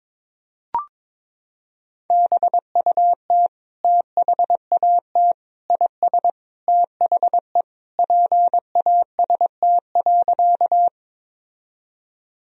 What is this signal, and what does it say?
Morse code, heard as a keyed tone, about 21 wpm, 700 Hz. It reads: BUT THAT IS THE PAST.
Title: Sentences from Top 1000 Words - Review Part 3 - 22wpm